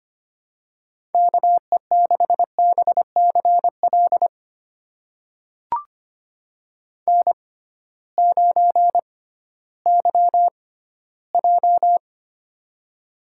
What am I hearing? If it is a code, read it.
Morse code, heard as a keyed tone, about 25 wpm, spaced as for 15 wpm, 700 Hz. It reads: KE6BCL N9YJ
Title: USA Call Signs - 6x Character Spacing - 25wpm